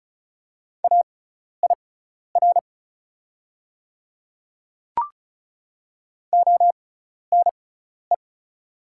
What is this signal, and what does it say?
Morse code, heard as a keyed tone, about 36 wpm, spaced as for 12 wpm, 690 Hz. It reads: AIR ONE
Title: 6x Character Spacing - Top 200 Words - 35wpm